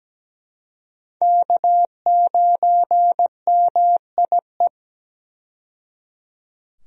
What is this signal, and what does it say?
Morse code, heard as a keyed tone, about 17 wpm, 700 Hz. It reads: K9MIE
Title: USA Call Signs - 5 Characters - 17wpm